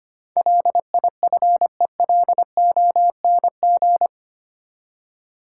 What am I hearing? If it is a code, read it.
Morse code, heard as a keyed tone, about 25 wpm, 700 Hz. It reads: LIFELONG
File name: Eight Letter Words - 25wpm